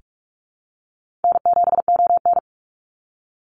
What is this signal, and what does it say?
Morse code, heard as a keyed tone, about 44 wpm, 700 Hz. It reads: N7ON
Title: Easy Call Signs - 45wpm